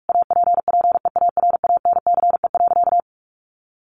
Morse code, heard as a keyed tone, about 44 wpm, 700 Hz. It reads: APPEARANCE.
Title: Sentences from Top 1100 Words 45wpm